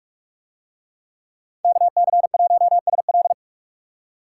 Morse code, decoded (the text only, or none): KC1SL